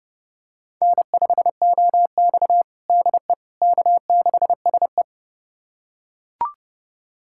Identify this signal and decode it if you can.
Morse code, heard as a keyed tone, about 30 wpm, 700 Hz. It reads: N5OX DE K6SE
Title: USA Callsign DE Callsign - 4 Characters - 30wpm